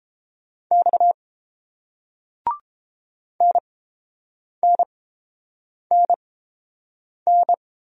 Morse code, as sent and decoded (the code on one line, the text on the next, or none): -..- / -. -. -. -.
X NNNN